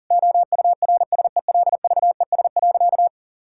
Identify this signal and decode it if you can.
Morse code, heard as a keyed tone, about 37 wpm, 700 Hz. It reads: OURSELVES.
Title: Sentences from Top 1600 Words 40wpm